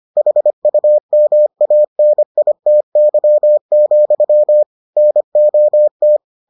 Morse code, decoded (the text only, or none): HUMANITY, NOT